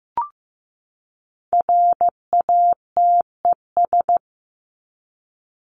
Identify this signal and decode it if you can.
Morse code, heard as a keyed tone, about 15 wpm, 700 Hz. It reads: RATES
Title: ES:S Suffix - Top 100 Words with Base Words 15wpm